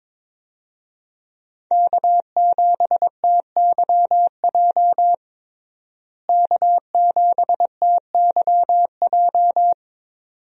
Morse code, heard as a keyed tone, about 22 wpm, 700 Hz. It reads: K7TYJ K7TYJ
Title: USA Call Signs - 5 Characters - Repeated 2x - 22wpm